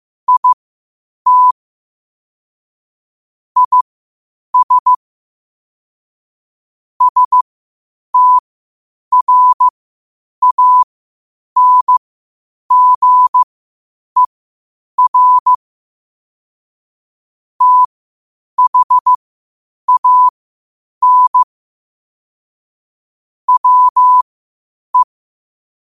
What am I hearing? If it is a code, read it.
Morse code, heard as a keyed tone, about 15 wpm, spaced as for 8 wpm, 1000 Hz. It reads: IT IS STRANGER THAN WE
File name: qotdfiles / 8wpm